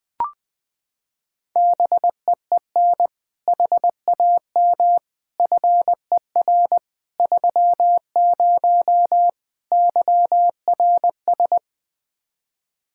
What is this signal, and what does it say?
Morse code, heard as a keyed tone, about 20 wpm, 700 Hz. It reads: BEEN HAM FER 30 YRS
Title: K1USN SST QSO Practice 20wpm